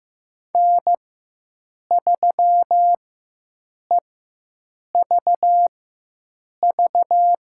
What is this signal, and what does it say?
Morse code, heard as a keyed tone, about 16 wpm, spaced as for 7 wpm, 700 Hz. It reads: N3EVV
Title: USA Call Signs - 5 Characters - 4x Character Spacing - 15wpm